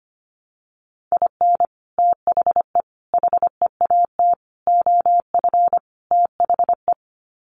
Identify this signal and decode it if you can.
Morse code, heard as a keyed tone, about 25 wpm, 700 Hz. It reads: IN THE HEAT OF THE
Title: Sets of 5 Words Encore2 25wpm